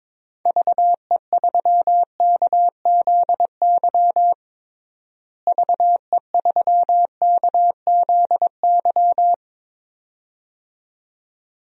Morse code, heard as a keyed tone, about 21 wpm, 700 Hz. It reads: VE3KZY VE3KZY
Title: DXCC-100 Call Signs - 6 Characters - Repeated 2x - 22wpm